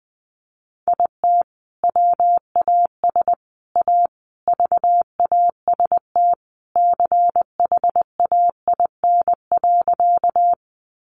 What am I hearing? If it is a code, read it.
Morse code, heard as a keyed tone, about 20 wpm, 700 Hz. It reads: IT WAS A VAST CHAIN.
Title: Sentences from Top 1500 Words 20wpm